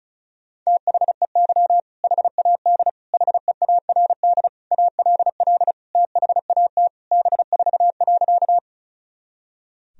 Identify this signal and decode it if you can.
Morse code, heard as a keyed tone, about 36 wpm, 700 Hz. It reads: THEY HAD HEARD ALL THAT B4.
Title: Sentences from Top 400 Words and Abbreviations 35wpm